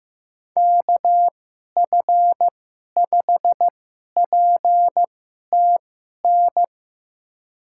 Morse code, heard as a keyed tone, about 15 wpm, spaced as for 11 wpm, 700 Hz. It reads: KF5PTN
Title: USA Call Signs - 6 Characters - 2x Character Spacing - 15wpm